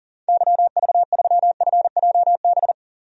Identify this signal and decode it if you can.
Morse code, heard as a keyed tone, about 40 wpm, 700 Hz. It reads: YV3FJB
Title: DXCC-100 Call Signs - 6 Characters - 40wpm